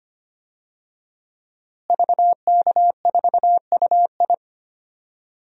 Morse code, heard as a keyed tone, about 25 wpm, 700 Hz. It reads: VK4UI
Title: DXCC-100 Call Signs - 5 Characters - Repeated 2x - 25wpm